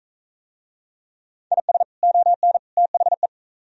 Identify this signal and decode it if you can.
Morse code, heard as a keyed tone, about 38 wpm, 700 Hz. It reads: IS ON THE